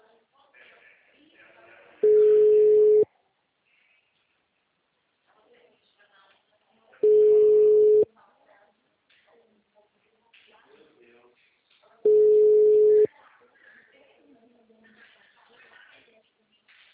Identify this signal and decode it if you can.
Ringback tone (Europe, 425 Hz), with speech.